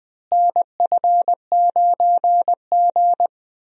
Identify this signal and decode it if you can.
Morse code, heard as a keyed tone, about 20 wpm, 700 Hz. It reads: NF9G
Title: USA Call Signs - 4 Characters - Repeated 3x - 20wpm